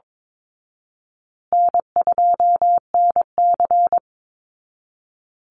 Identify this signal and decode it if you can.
Morse code, heard as a keyed tone, about 22 wpm, 700 Hz. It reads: N2NC
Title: Easy Call Signs Repeated 3x - 22wpm